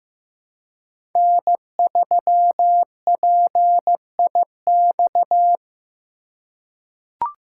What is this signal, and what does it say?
Morse code, heard as a keyed tone, about 15 wpm, 700 Hz. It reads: N3PIX
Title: USA Call Signs - 5 Characters - 2x Character Spacing - 15wpm